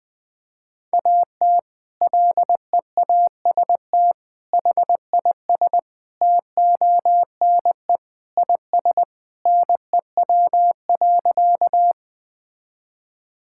Morse code, heard as a keyed tone, about 20 wpm, 700 Hz. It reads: AT LEAST HIS TONE IS NEW.
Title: Sentences from Top 1600 Words 20wpm